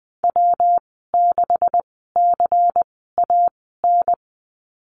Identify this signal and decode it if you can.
Morse code, heard as a keyed tone, about 20 wpm, spaced as for 15 wpm, 700 Hz. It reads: W6CAN
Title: 128 Lesson 21g - Number 6 - Call Signs 20wpm